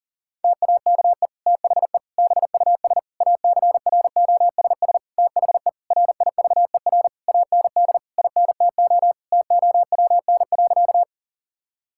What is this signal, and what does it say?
Morse code, heard as a keyed tone, about 40 wpm, 700 Hz. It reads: TAKE THE BUS ACROSS THE RIVER AND INTO TOWN.
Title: Sentences from Top 1000 Words - Review Part 6 - 40wpm